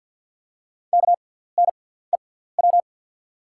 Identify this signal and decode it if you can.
Morse code, heard as a keyed tone, about 48 wpm, spaced as for 17 wpm, 690 Hz. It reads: KNEW
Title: 6x Character Spacing - Top 500 Words - 50wpm